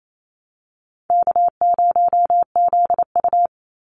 Morse code, heard as a keyed tone, about 28 wpm, 700 Hz. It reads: K0ZU
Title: Easy Call Signs Repeated 2x - 28wpm